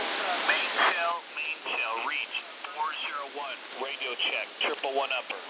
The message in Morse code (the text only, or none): none